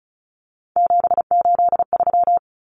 Morse code, heard as a keyed tone, about 34 wpm, 700 Hz. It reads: 783
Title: Numbers - 3-Digit 35wpm